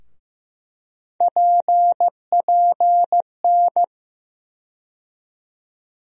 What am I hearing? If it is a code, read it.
Morse code, heard as a keyed tone, about 15 wpm, 700 Hz. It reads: PPN
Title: USA Call Sign Postfixes - 3 Letters 15wpm